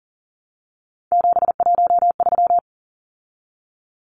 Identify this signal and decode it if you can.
Morse code, heard as a keyed tone, about 40 wpm, 700 Hz. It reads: 713
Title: Numbers - 3-Digit 40wpm